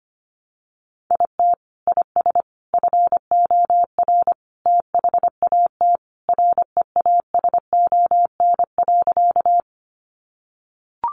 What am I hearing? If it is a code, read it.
Morse code, heard as a keyed tone, about 25 wpm, 700 Hz. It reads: IT IS FOR THAT REASON.
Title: Sentences from Top 700 Words 25wpm